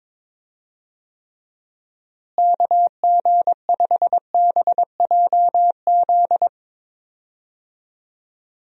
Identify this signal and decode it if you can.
Morse code, heard as a keyed tone, about 21 wpm, 700 Hz. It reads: KG5BJZ
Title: USA Call Signs - 6 Characters - Repeated 2x - 22wpm